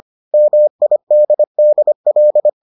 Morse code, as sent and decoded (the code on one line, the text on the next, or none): -- .. -.. -.. .-..
MIDDL